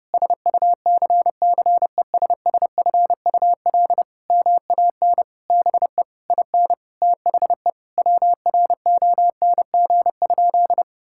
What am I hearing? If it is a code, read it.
Morse code, heard as a keyed tone, about 30 wpm, 700 Hz. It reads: SUCCESSFUL MAN BE IN THE WRONG?
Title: Sentences from Top 1000 Words - Review Part 6 - 30wpm